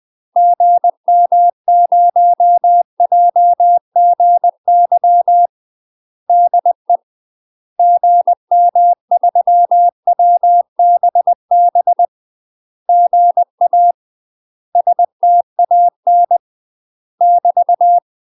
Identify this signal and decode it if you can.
Morse code, heard as a keyed tone, about 20 wpm, 700 Hz. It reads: GM0JGY DE GM3WBB GA STAN <BT>
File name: UK MASTER QSOs TRAINING / 20WPM / UK MASTER QSOs 0058